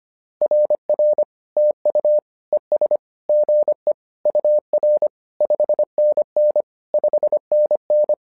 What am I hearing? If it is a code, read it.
Morse code, heard as a keyed tone, about 25 wpm, 600 Hz. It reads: RR TU ES GE UR 5NN 5NN
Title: SOTA QSO Practice 25wpm